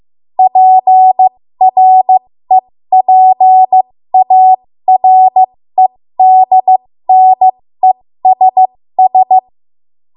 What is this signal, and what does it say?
Morse code, heard as a keyed tone, about 15 wpm, spaced as for 13 wpm, 750 Hz. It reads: PREPAREDNESS
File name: Morse / 13WPM